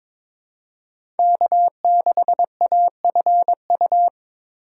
Morse code, heard as a keyed tone, about 21 wpm, 700 Hz. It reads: K6AFU
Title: USA Call Signs - 2x Character Spacing - 22wpm